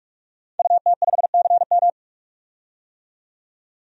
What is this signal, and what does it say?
Morse code, heard as a keyed tone, about 42 wpm, 700 Hz. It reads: UT5CM
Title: DXCC-100 Call Signs - 5 Characters - 45wpm